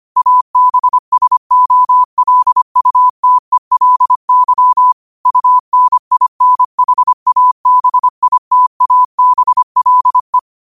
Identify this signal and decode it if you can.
Morse code, heard as a keyed tone, about 25 wpm, 1000 Hz. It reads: ABSOLUTELY UNINHABITABLE
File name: qotdfiles / 25wpm